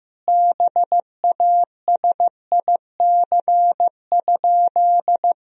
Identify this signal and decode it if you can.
Morse code, heard as a keyed tone, about 15 wpm, 700 Hz. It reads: BASIC?
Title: Sentences from Top 1000 Words - Review Part 3 - 15wpm